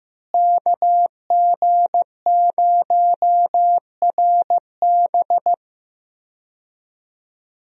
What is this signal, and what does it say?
Morse code, heard as a keyed tone, about 16 wpm, 700 Hz. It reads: KG0RB
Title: USA Call Signs - 5 Characters - Repeated 3x - 15wpm